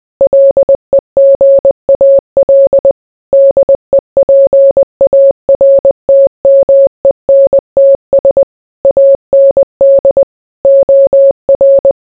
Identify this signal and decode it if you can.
Morse code, heard as a keyed tone, about 20 wpm, 550 Hz. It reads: LEGAL DEPARTMENTS AND OR